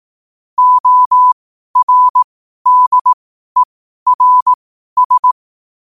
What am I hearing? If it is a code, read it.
Morse code, heard as a keyed tone, about 18 wpm, spaced as for 12 wpm, 1000 Hz. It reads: ORDERS
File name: qotdfiles / 12wpm